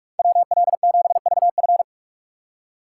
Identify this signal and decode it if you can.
Morse code, heard as a keyed tone, about 46 wpm, 700 Hz. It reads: WL7VF
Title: USA Call Signs - 5 Characters - 3x Character Spacing - 45wpm